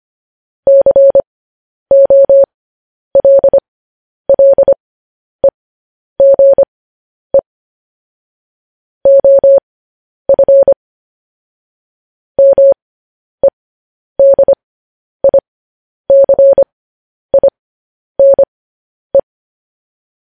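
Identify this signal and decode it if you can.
Morse code, heard as a keyed tone, about 25 wpm, spaced as for 10 wpm, 550 Hz. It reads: COLLEGE OF MEDICINE
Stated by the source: Morse code news headlines at 10.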